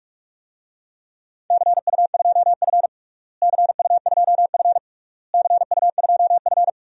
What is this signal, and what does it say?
Morse code, heard as a keyed tone, about 44 wpm, 700 Hz. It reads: CU2F CU2F CU2F
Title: DXCC-100 Call Signs - 4 Characters - Repeated 3x - 45wpm